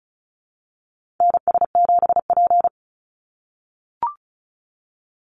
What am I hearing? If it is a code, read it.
Morse code, heard as a keyed tone, about 34 wpm, 700 Hz. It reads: NS7P
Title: NA SOTA Activator Callsigns 35wpm